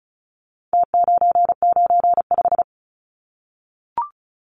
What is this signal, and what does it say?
Morse code, heard as a keyed tone, about 34 wpm, 700 Hz. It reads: T995
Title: Numbers - Contest Style - 4-Digit 35wpm